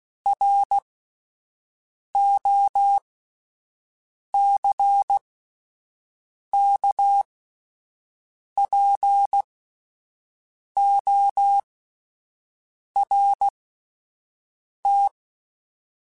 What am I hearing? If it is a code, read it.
Morse code, heard as a keyed tone, about 16 wpm, spaced as for 5 wpm, 800 Hz. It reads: ROCKPORT